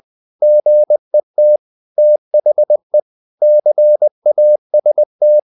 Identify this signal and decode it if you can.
Morse code, heard as a keyed tone, about 20 wpm, 600 Hz. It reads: GET THE CAST
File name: the old man and the sea / the old man and the sea part 1 20wpm